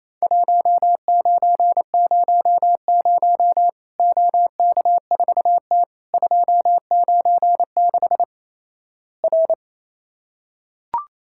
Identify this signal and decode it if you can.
Morse code, heard as a keyed tone, about 28 wpm, 700 Hz. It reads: 1900 OK4T 296 R
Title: Worked All Europe DX Contest - QTC Exchange - 28wpm